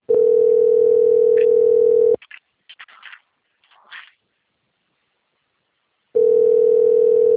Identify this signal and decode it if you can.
Ringback tone (North America), 440 + 480 Hz.
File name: out- - - - - .20372